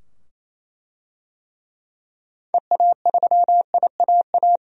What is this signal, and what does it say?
Morse code, heard as a keyed tone, about 28 wpm, 700 Hz. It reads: EA3IAA
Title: DXCC-100 Call Signs - 6 Characters - 28wpm